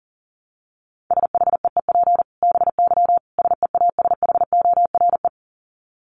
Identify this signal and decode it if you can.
Morse code, heard as a keyed tone, about 40 wpm, 700 Hz. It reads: SHEEP BY SEASHORE
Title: Dit Challenge 40wpm